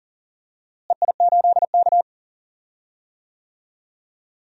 Morse code, heard as a keyed tone, about 39 wpm, 700 Hz. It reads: EI8K
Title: DXCC-100 Call Signs - 4 Characters - Repeated 3x - 40wpm